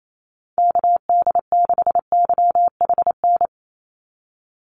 Morse code, heard as a keyed tone, about 28 wpm, 700 Hz. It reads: KD6YHN
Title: USA Call Signs 28wpm